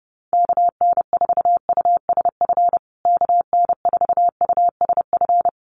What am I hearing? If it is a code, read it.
Morse code, heard as a keyed tone, about 30 wpm, 700 Hz. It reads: KN4USF KN4USF
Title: USA Call Signs Repeated 2x 30wpm